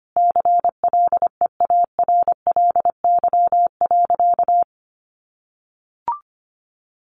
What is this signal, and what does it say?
Morse code, heard as a keyed tone, about 25 wpm, 700 Hz. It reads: CLEARLY.
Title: Sentences from Top 1300 Words 25wpm